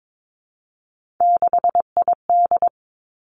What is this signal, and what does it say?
Morse code, heard as a keyed tone, about 22 wpm, 700 Hz. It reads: 6ID
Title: Three Challenging Letter-Number Combinations 22wpm